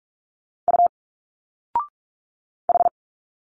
Morse code, heard as a keyed tone, about 46 wpm, 700 Hz. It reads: U H
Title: Single Letters 45wpm